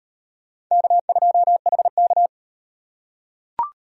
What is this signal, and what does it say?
Morse code, heard as a keyed tone, about 38 wpm, 700 Hz. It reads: K2HK